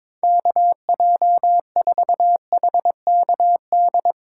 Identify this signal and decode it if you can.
Morse code, heard as a keyed tone, about 23 wpm, 700 Hz. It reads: KJ4HKD
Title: Speed Racing - USA 2x3 Call Signs 22wpm